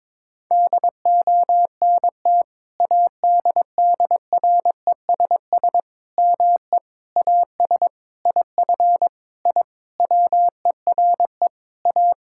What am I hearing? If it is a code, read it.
Morse code, heard as a keyed tone, about 22 wpm, 700 Hz. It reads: DONT ADDRESS ME AS IF I WERE A
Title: Sentences from Top 1000 Words - Review Part 6 - 22wpm